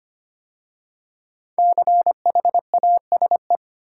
Morse code, heard as a keyed tone, about 25 wpm, 700 Hz. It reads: CHASE
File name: 5 Letter Words - First 3 Letters Given - 25wpm